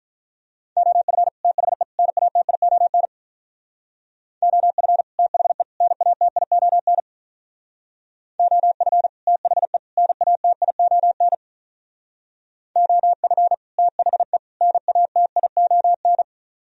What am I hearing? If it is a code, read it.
Morse code, heard as a keyed tone, about 46 wpm, 700 Hz. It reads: OF THE NATION OF THE NATION OF THE NATION OF THE NATION